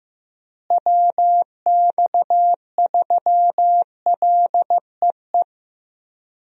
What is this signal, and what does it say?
Morse code, heard as a keyed tone, about 15 wpm, 700 Hz. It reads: WX3LEE
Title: USA Call Signs - 6 Characters - Repeated 2x - 15wpm